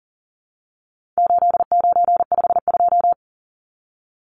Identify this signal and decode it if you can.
Morse code, heard as a keyed tone, about 40 wpm, 700 Hz. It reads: 8952
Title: Numbers - 4-Digit 40wpm